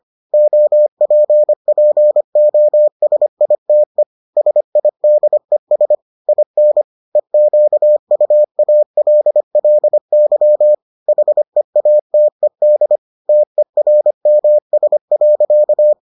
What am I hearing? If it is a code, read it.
Morse code, heard as a keyed tone, about 25 wpm, 600 Hz. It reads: OPPOSITE SIDES IN EQUALLY HEATED TERMS.